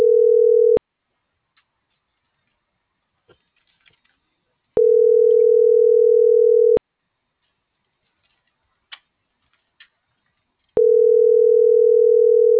Ringback tone (North America), 440 + 480 Hz.